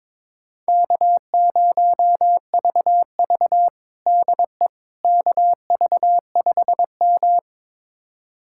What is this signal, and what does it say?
Morse code, heard as a keyed tone, about 22 wpm, 700 Hz. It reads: K0VV DE KV5M
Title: USA Callsign DE Callsign - 4 Characters - 22wpm